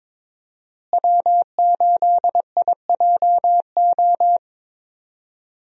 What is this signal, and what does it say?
Morse code, heard as a keyed tone, about 22 wpm, 700 Hz. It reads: W8IJO
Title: USA Call Signs - 5 Characters - Repeated 3x - 22wpm